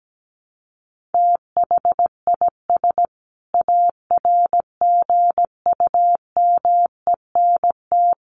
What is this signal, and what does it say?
Morse code, heard as a keyed tone, about 17 wpm, 700 Hz. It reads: THIS ARGUMENT
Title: Sentences from Top 1400 Words 17wpm